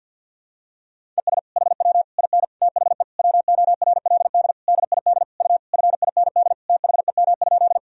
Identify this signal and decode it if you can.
Morse code, heard as a keyed tone, about 52 wpm, 700 Hz. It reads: ES HW IN THE WORLD DID U FIND THEM?
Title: Sentences from Top 200 Words and Abbreviations 50wpm